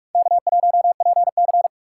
Morse code, heard as a keyed tone, about 42 wpm, 700 Hz. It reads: K1PC
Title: USA Call Signs - 4 Characters - 3x Character Spacing - 45wpm